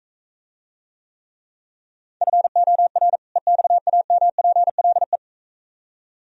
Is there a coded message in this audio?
Yes: Morse code from 2.2 to 5.2 s, heard as a keyed tone, about 44 wpm, 700 Hz. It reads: FOR EXAMPLE